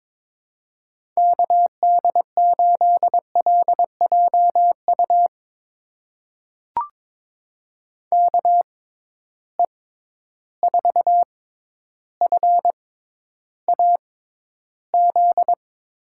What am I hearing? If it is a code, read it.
Morse code, heard as a keyed tone, about 21 wpm, spaced as for 11 wpm, 700 Hz. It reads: KD8LJU KE4FAZ
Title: USA Call Signs - 6 Characters - 6x Character Spacing - 22wpm